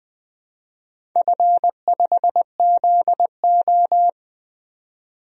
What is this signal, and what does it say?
Morse code, heard as a keyed tone, about 20 wpm, 700 Hz. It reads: F5ZO
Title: DXCC-100 Call Signs - 4 Characters - 20wpm